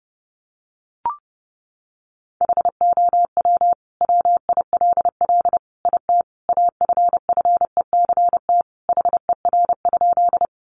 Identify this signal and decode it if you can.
Morse code, heard as a keyed tone, about 30 wpm, 700 Hz. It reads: HOW WILL IT AFFECT HER?
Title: Sentences from Top 900 Words 30wpm